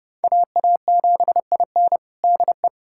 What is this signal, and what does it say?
Morse code, heard as a keyed tone, about 30 wpm, 700 Hz. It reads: AA7IN DE
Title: USA Callsign DE Callsign 30wpm